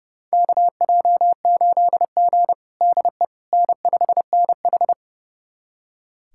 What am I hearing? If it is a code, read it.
Morse code, heard as a keyed tone, about 32 wpm, 700 Hz. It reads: KJ8G DE N5NH
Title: USA Callsign DE Callsign - 4 Characters - 30wpm